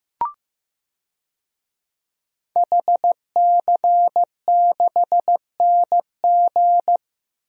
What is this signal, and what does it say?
Morse code, heard as a keyed tone, about 15 wpm, 700 Hz. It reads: HC6NG
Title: DXCC-100 Call Signs - 5 Characters - 15wpm